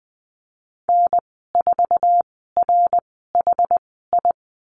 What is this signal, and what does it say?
Morse code, heard as a keyed tone, about 20 wpm, spaced as for 14 wpm, 700 Hz. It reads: N4RHI
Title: 069 Lesson 12g - Letter d - Call Signs 20wpm